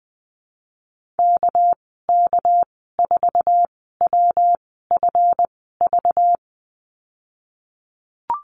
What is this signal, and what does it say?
Morse code, heard as a keyed tone, about 19 wpm, spaced as for 14 wpm, 700 Hz. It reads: KK4WFV
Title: 219 Lesson 35g - Number 0 - Call Signs 20wpm